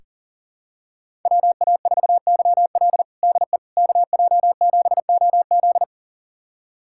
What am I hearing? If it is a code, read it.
Morse code, heard as a keyed tone, about 40 wpm, 700 Hz. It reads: WA4YL DE KJ7OZ
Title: USA Callsign DE Callsign 40wpm